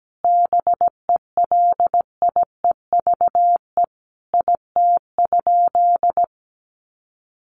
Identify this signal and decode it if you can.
Morse code, heard as a keyed tone, about 17 wpm, 700 Hz. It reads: BELIEVE IT?
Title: Sentences from Top 1000 Words 17wpm